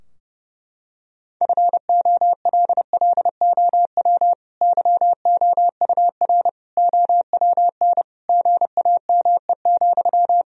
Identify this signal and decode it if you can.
Morse code, heard as a keyed tone, about 32 wpm, 700 Hz. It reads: FOLLOW YOUR OWN GAME,
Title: Sentences from Top 1000 Words - Review Part 3 - 30wpm